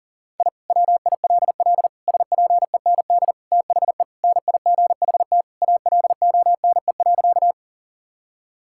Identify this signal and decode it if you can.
Morse code, heard as a keyed tone, about 37 wpm, 700 Hz. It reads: I WILL SPEND THE NIGHT ALONE.
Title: Sentences from Top 1000 Words - Review Part 5 - 40wpm